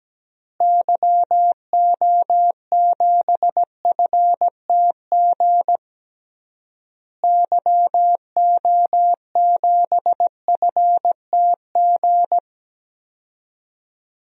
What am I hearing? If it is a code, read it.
Morse code, heard as a keyed tone, about 17 wpm, 700 Hz. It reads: YO7FTG YO7FTG
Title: DXCC-100 Call Signs - 6 Characters - Repeated 3x - 17wpm